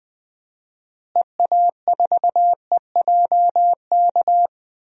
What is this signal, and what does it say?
Morse code, heard as a keyed tone, about 20 wpm, 700 Hz. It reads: EA4EJK
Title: DXCC-100 Call Signs - 6 Characters - Repeated 2x - 20wpm